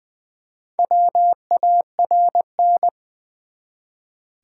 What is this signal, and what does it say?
Morse code, heard as a keyed tone, about 20 wpm, 700 Hz. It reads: WARN
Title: Build A Word - 1 Letter At A Time - 4-Letter Words - 20wpm